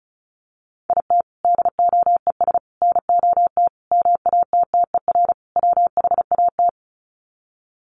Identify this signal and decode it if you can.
Morse code, heard as a keyed tone, about 34 wpm, 700 Hz. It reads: IT DOES NOT MATTER WHAT
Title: Sets of 5 Words Encore 35wpm